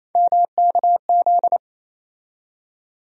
Morse code, heard as a keyed tone, about 28 wpm, 700 Hz. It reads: MKZ
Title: USA Call Sign Postfixes - 3 Letters 28wpm